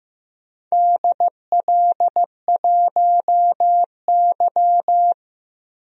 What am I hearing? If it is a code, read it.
Morse code, heard as a keyed tone, about 15 wpm, 700 Hz. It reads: DL1Y
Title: DXCC-100 Call Signs - 4 Characters - 15wpm